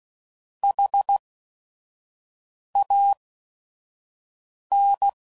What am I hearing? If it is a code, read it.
Morse code, heard as a keyed tone, about 16 wpm, spaced as for 5 wpm, 800 Hz. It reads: HAN
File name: CW Code Groups @5WPM / Technical Subjects / XC